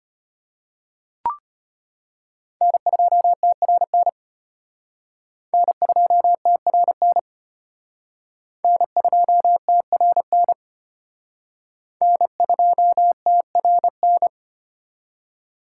Morse code, heard as a keyed tone, about 33 wpm, 700 Hz. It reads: N2TRN N2TRN N2TRN N2TRN